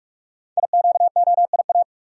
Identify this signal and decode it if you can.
Morse code, heard as a keyed tone, about 44 wpm, 700 Hz. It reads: IQOIA
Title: Code Groups - 5 Letters - 45wpm